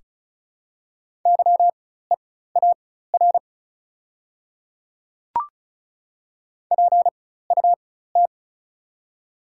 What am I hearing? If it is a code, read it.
Morse code, heard as a keyed tone, about 36 wpm, spaced as for 16 wpm, 690 Hz. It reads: YEAR PUT
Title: 4x Character Spacing - Top 300 Words - 35wpm